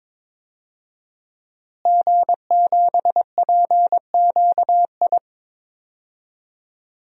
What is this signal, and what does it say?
Morse code, heard as a keyed tone, about 22 wpm, 700 Hz. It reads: G7PQI
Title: DXCC-100 Call Signs - 5 Characters - Repeated 2x - 22wpm